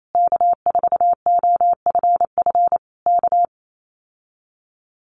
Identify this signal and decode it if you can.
Morse code, heard as a keyed tone, about 28 wpm, 700 Hz. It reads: K4OFF K
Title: Common QSO Phrases Encore 28wpm